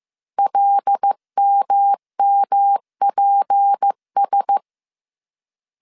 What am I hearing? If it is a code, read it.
Morse code, heard as a keyed tone, about 15 wpm, 800 Hz. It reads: LMMPS
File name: CW / DET / 1 6 1500C S C 14wpm 2951